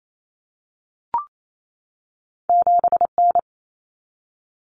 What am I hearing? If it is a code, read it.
Morse code, heard as a keyed tone, about 28 wpm, 700 Hz. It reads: 7N
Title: Common Cut Numbers - 2-Digit 28wpm